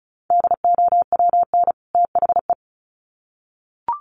Morse code, heard as a keyed tone, about 34 wpm, 700 Hz. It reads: DOWN THE
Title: Sets of 2 Words 35wpm